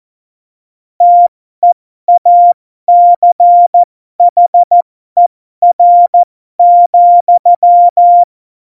Morse code, heard as a keyed tone, about 14 wpm, spaced as for 12 wpm, 700 Hz. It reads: TEACHER,
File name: Morsekurs / Witze